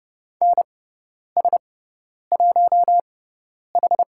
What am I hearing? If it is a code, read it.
Morse code, heard as a keyed tone, about 29 wpm, spaced as for 10 wpm, 700 Hz. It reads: NS1H
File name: Lesson 18g - Letter m - Call Signs 30wpm